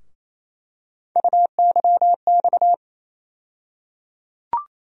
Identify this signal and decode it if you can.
Morse code, heard as a keyed tone, about 28 wpm, 700 Hz. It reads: UYX
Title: USA Call Sign Postfixes - 3 Letters 28wpm